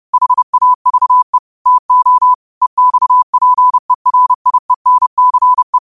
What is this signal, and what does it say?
Morse code, heard as a keyed tone, about 30 wpm, 1000 Hz. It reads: HAVE TO EXPERIENCE
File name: 30wpm